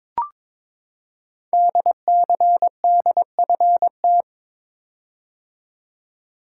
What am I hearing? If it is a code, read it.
Morse code, heard as a keyed tone, about 22 wpm, 700 Hz. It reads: DCDFT
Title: Code Groups - 5 Letters - 22wpm